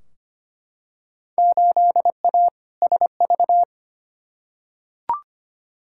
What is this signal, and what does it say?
Morse code, heard as a keyed tone, about 25 wpm, 700 Hz. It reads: 8A SV
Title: Field Day Exchanges Encore - 25wpm